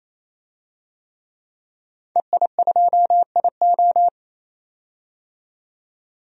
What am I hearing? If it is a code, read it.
Morse code, heard as a keyed tone, about 28 wpm, 700 Hz. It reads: EI2IO
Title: DXCC-100 Call Signs - 5 Characters - 28wpm